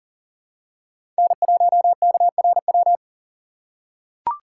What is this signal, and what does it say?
Morse code, heard as a keyed tone, about 40 wpm, 700 Hz. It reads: N1KRW
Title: USA Call Signs - 6x Character Spacing - 40wpm